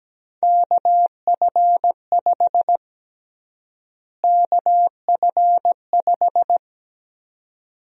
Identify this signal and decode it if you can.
Morse code, heard as a keyed tone, about 17 wpm, 700 Hz. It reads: KF5 KF5
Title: USA Call Sign Prefixes - Repeated 2x - 17wpm